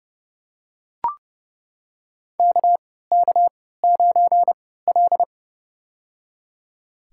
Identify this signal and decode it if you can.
Morse code, heard as a keyed tone, about 32 wpm, spaced as for 17 wpm, 700 Hz. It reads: KK9L
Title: USA Call Signs - 4 Characters - 3x Character Spacing - 30wpm